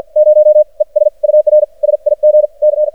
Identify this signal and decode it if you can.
Morse code, heard as a keyed tone, about 53 wpm, spaced as for 44 wpm, 610 Hz. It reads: 0 ES UR SIG C
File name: archived / CRSnet